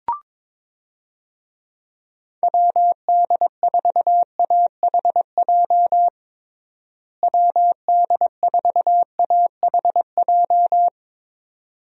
Morse code, heard as a keyed tone, about 21 wpm, 700 Hz. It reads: WD4AHJ WD4AHJ
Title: USA Call Signs - 6 Characters - Repeated 2x - 22wpm